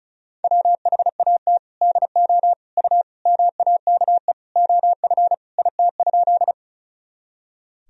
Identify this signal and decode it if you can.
Morse code, heard as a keyed tone, about 36 wpm, 700 Hz. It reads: WHAT DO U MAKE OF IT?
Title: Sentences from Top 100 Words and Abbreviations 35wpm